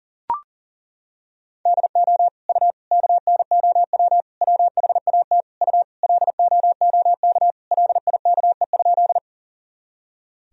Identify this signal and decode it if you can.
Morse code, heard as a keyed tone, about 40 wpm, 700 Hz. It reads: DO U KNOW WHAT U LOOK LIKE?
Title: Sentences from Top 100 Words and Abbreviations 40wpm